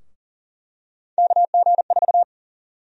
Morse code, heard as a keyed tone, about 40 wpm, 700 Hz. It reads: KG4
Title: USA Call Sign Prefixes - 40wpm